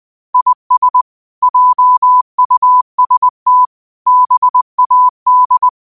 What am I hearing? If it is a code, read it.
Morse code, heard as a keyed tone, about 20 wpm, 1000 Hz. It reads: IS JUST BAD
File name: qotdfiles / 20wpm